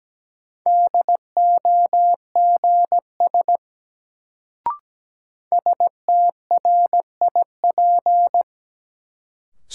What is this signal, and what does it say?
Morse code, heard as a keyed tone, about 17 wpm, 700 Hz. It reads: DOGS STRIP
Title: Sentences from Top 1600 Words - New Word Drill - 17wpm